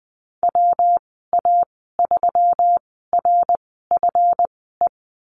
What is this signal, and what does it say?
Morse code, heard as a keyed tone, about 20 wpm, spaced as for 15 wpm, 700 Hz. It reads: WA3RFE
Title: 147 Lesson 24g - Letter y - Call Signs 20wpm